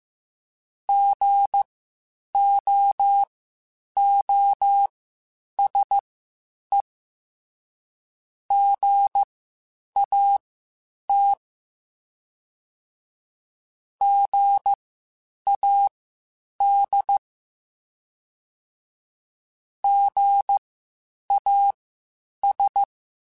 Morse code, heard as a keyed tone, about 15 wpm, spaced as for 8 wpm, 800 Hz. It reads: GOOSE GAT GAD GAS
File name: Spaced Alphas / Spaced Alphas mp3s / Gs sp @13